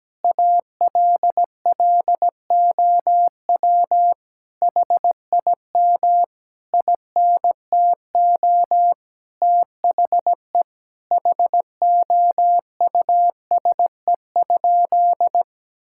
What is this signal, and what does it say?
Morse code, heard as a keyed tone, about 17 wpm, 700 Hz. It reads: ALLOW HIM INTO THE HOUSE?
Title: Sentences from Top 1000 Words - Review Part 5 - 17wpm